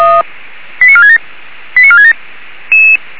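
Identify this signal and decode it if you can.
Five-tone selective call (ZVEI), heard as quick runs of five tones, tones of about 70 ms. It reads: 89467 89467
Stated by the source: ZVEI five-tone selective call received with an SDR, hosted by Feuerwehr Orlamünde (fire brigade).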